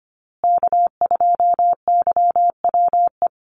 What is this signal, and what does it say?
Morse code, heard as a keyed tone, about 25 wpm, 700 Hz. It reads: K2YWE
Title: Call Signs Repeated 2x - 25wpm